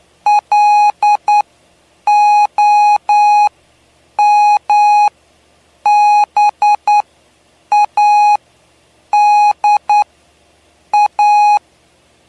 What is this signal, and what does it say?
Morse code, heard as a keyed tone, about 9 wpm, spaced as for 7 wpm, 830 Hz. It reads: LOMBADA
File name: cw / py2lcd / 07-PY2LCD-Li��o 7